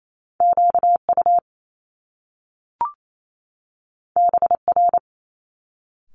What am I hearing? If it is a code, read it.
Morse code, heard as a keyed tone, about 28 wpm, 700 Hz. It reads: QU BR
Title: Top Two Letter Combinations in Words 28wpm